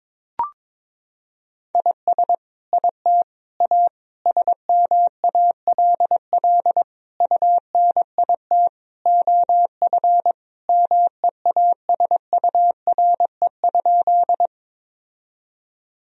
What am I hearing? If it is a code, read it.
Morse code, heard as a keyed tone, about 22 wpm, 700 Hz. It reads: IS IT A SMALL UNIT OF MEASURE?
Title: Sentences from Top 1000 Words - Review Part 2 - 22wpm